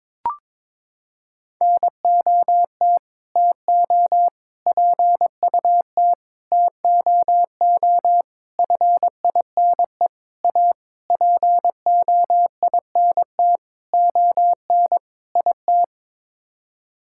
Morse code, heard as a keyed tone, about 22 wpm, 700 Hz. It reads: NOT TO PUT TOO FINE A POINT ON IT
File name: Sets of 9 Words - 22wpm